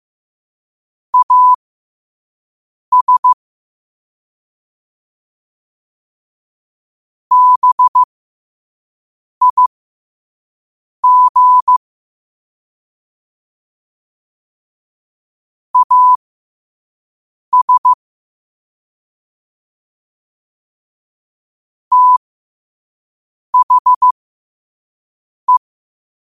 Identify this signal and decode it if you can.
Morse code, heard as a keyed tone, about 15 wpm, spaced as for 5 wpm, 1000 Hz. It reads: AS BIG AS THE
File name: qotdfiles / 5wpm